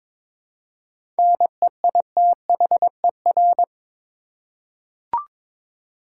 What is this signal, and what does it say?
Morse code, heard as a keyed tone, about 22 wpm, 700 Hz. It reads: NEITHER
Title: Seven Letter Words - 22wpm